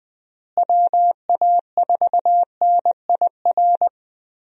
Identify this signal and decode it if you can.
Morse code, heard as a keyed tone, about 20 wpm, 700 Hz. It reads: WA4NIR
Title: USA Call Signs - 6 Characters - Repeated 3x - 20wpm